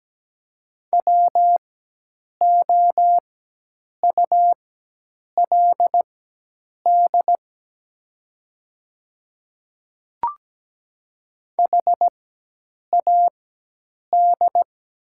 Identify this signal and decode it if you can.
Morse code, heard as a keyed tone, about 17 wpm, spaced as for 8 wpm, 700 Hz. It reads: WOULD HAD
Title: 4x Character Spacing - Top 100 Words - 17wpm